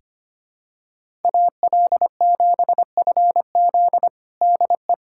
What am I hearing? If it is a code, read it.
Morse code, heard as a keyed tone, about 25 wpm, 700 Hz. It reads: AL7FZ DE
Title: USA Callsign DE Callsign - 5 Characters - 25wpm